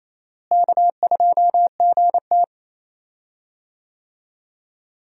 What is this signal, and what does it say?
Morse code, heard as a keyed tone, about 28 wpm, 700 Hz. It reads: K2GT
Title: USA Call Signs - 4 Characters - Repeated 3x - 28wpm